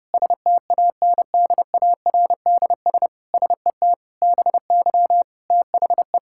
Morse code, heard as a keyed tone, about 30 wpm, 700 Hz. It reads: STANDARDS SET BY THE
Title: Sentences from Top 1000 Words - Review Part 6 - 30wpm